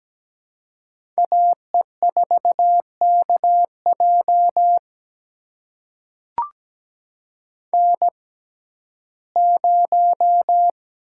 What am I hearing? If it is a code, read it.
Morse code, heard as a keyed tone, about 17 wpm, spaced as for 12 wpm, 700 Hz. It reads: AE4KJ N0
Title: USA Call Signs - 5 Characters - 6x Character Spacing - 17wpm